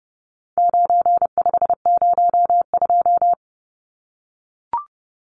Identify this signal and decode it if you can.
Morse code, heard as a keyed tone, about 30 wpm, 700 Hz. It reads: 9502
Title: Numbers - 4-Digit 30wpm